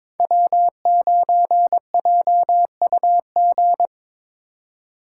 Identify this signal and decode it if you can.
Morse code, heard as a keyed tone, about 22 wpm, 700 Hz. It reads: W9JUG
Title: K1USN SST QSO Practice - 22wpm